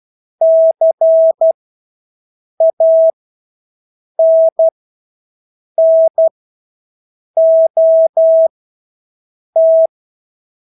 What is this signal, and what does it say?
Morse code, heard as a keyed tone, about 12 wpm, spaced as for 6 wpm, 650 Hz. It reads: CANNOT
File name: Engelske ord 12-6 wpm